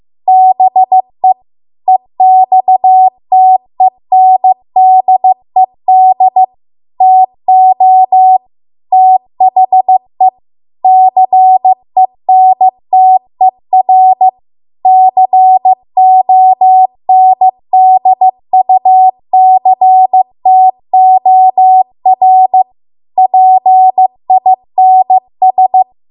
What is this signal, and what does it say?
Morse code, heard as a keyed tone, about 15 wpm, 750 Hz. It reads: BE EXTENDED TO THE CENTER CONDUCTOR PINS
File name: Archive / 15 WPM / 15WPM